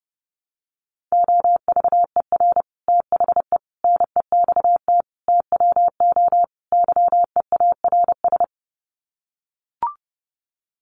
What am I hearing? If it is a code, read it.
Morse code, heard as a keyed tone, about 30 wpm, 700 Hz. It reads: OVER THE NEXT TWO YEARS
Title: Sets of 5 Words Encore2 30wpm